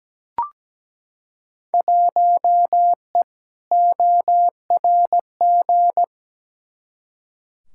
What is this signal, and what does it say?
Morse code, heard as a keyed tone, about 17 wpm, 700 Hz. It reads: 1E ORG
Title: Field Day Exchanges - 17wpm